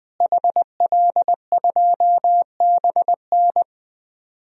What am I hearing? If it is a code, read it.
Morse code, heard as a keyed tone, about 20 wpm, 700 Hz. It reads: HL2BN
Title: DXCC-100 Call Signs - 5 Characters - Repeated 2x - 20wpm